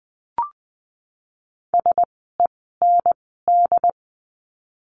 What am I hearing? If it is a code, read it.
Morse code, heard as a keyed tone, about 20 wpm, spaced as for 15 wpm, 700 Hz. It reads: SEND
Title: 068 Lesson 12f - Letter d - Words 20wpm